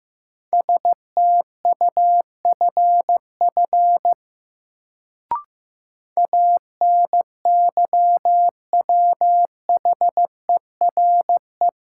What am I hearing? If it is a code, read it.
Morse code, heard as a keyed tone, about 16 wpm, 700 Hz. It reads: STUFF ANYWHERE
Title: Sentences from Top 1400 Words - New Word Drill - 15wpm